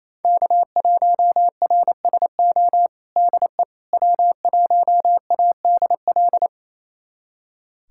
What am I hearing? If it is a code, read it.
Morse code, heard as a keyed tone, about 28 wpm, 700 Hz. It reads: K1RSO DE W1ADL
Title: USA Callsign DE Callsign - 5 Characters - 28wpm